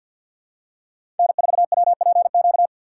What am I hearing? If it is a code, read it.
Morse code, heard as a keyed tone, about 48 wpm, 690 Hz. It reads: N4WPQ
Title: USA Call Signs - 4x Character Spacing - 50wpm